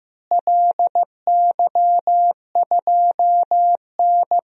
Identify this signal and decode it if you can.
Morse code, heard as a keyed tone, about 16 wpm, 700 Hz. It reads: LY2N
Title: DXCC-100 Call Signs - 4 Characters - Repeated 2x - 15wpm